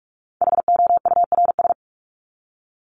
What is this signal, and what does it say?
Morse code, heard as a keyed tone, about 44 wpm, 700 Hz. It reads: HOURS
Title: ES:S Suffix - Top 100 Words with Base Words 45wpm